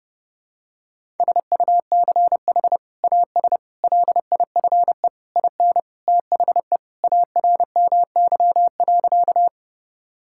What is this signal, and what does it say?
Morse code, heard as a keyed tone, about 30 wpm, 700 Hz. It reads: SUCH AS LIFE IN THE ARMY.
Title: Sentences from Top 1000 Words - Review Part 5 - 30wpm